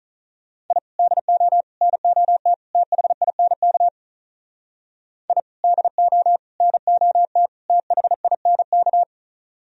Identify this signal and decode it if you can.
Morse code, heard as a keyed tone, about 43 wpm, 700 Hz. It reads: I DO NOT THINK I DO NOT THINK
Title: Speed-Racing - Sets of 4 Words 35wpm